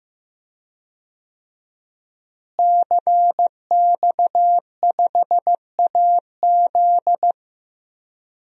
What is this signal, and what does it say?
Morse code, heard as a keyed tone, about 15 wpm, 700 Hz. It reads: CX5AZ
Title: DXCC-100 Call Signs - 5 Characters - Repeated 3x - 15wpm